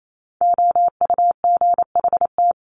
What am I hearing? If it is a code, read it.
Morse code, heard as a keyed tone, about 28 wpm, 700 Hz. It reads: OUGHT
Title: Top 1751-2000 Words 28wpm